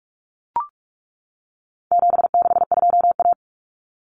Morse code, heard as a keyed tone, about 44 wpm, 700 Hz. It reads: 762A
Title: Common Cut Numbers - 4-Digit 45wpm